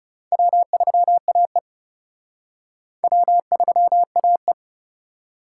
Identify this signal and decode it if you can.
Morse code, heard as a keyed tone, about 34 wpm, 700 Hz. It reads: W3AE W3AE
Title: Speed Racing - USA 1x2 Call Signs 30wpm